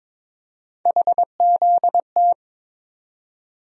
Morse code, heard as a keyed tone, about 22 wpm, 700 Hz. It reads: HZT
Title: USA Call Sign Postfixes - 3 Letters 22wpm